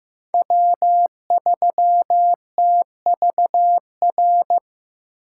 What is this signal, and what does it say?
Morse code, heard as a keyed tone, about 15 wpm, 700 Hz. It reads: W3TVR
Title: USA Call Signs - 8x Character Spacing - 15wpm